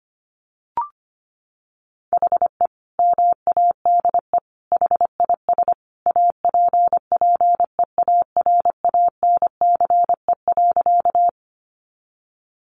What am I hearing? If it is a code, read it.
Morse code, heard as a keyed tone, about 25 wpm, 700 Hz. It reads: HE MADE HIS APPEARANCE.
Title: Sentences from Top 1100 Words 25wpm